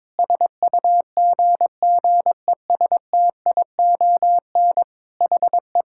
Morse code, heard as a keyed tone, about 22 wpm, 700 Hz. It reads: SUGGESTION HE
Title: Sentences from Top 1600 Words 22wpm